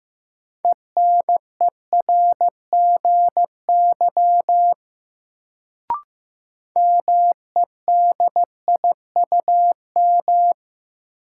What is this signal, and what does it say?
Morse code, heard as a keyed tone, about 15 wpm, 700 Hz. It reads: ENERGY MEDIUM
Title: Sentences from Top 900 Words - New Word Drill - 15wpm